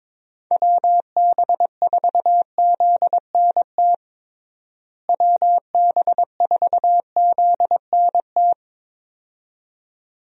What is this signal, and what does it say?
Morse code, heard as a keyed tone, about 22 wpm, 700 Hz. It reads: WB4ZNT WB4ZNT
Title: USA Call Signs - 6 Characters - Repeated 2x - 22wpm